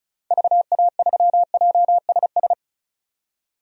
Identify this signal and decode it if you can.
Morse code, heard as a keyed tone, about 36 wpm, 700 Hz. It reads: VA3JSS
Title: DXCC-100 Call Signs - 6 Characters - 35wpm